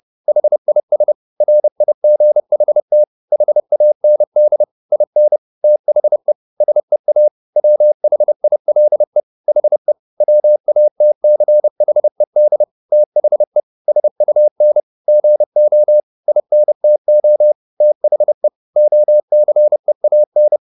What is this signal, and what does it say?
Morse code, heard as a keyed tone, about 30 wpm, 600 Hz. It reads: HIS RIGHT HAND IN THE SEA WHILE HE WATCHED THE SUN GO INTO THE OCEAN